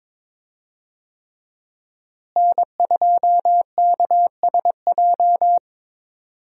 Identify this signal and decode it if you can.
Morse code, heard as a keyed tone, about 22 wpm, 700 Hz. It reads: N2KSJ
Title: USA Call Signs - 5 Characters - Repeated 3x - 22wpm